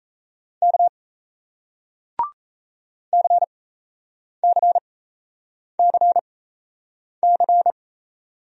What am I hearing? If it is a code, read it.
Morse code, heard as a keyed tone, about 38 wpm, spaced as for 8 wpm, 700 Hz. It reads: K CCCC